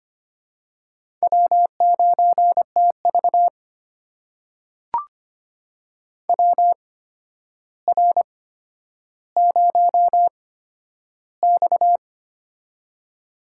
Morse code, heard as a keyed tone, about 25 wpm, spaced as for 11 wpm, 700 Hz. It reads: W9TV WR0X
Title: USA Call Signs - 4 Characters - 8x Character Spacing - 25wpm